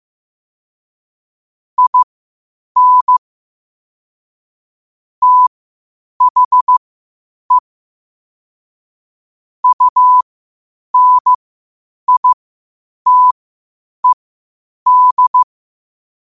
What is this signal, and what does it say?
Morse code, heard as a keyed tone, about 15 wpm, spaced as for 8 wpm, 1000 Hz. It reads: IN THE UNITED
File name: qotdfiles / 8wpm